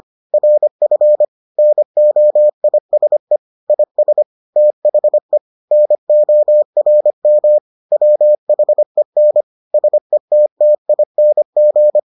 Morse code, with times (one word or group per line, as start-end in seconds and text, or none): 0.3-1.2 RF
1.6-3.4 NOISE
3.7-4.2 IS
4.6-5.4 THE
5.7-7.6 NORM
7.9-9.4 WHEN
9.7-12.0 SETTING